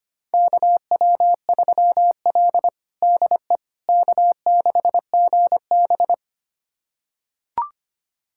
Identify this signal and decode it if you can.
Morse code, heard as a keyed tone, about 25 wpm, 700 Hz. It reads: KW3L DE K6GB
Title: USA Callsign DE Callsign - 4 Characters - 25wpm